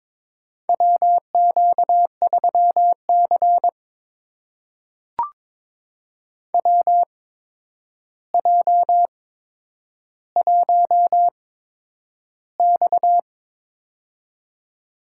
Morse code, heard as a keyed tone, about 22 wpm, spaced as for 9 wpm, 700 Hz. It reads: WQ3C WJ1X
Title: USA Call Signs - 4 Characters - 8x Character Spacing - 22wpm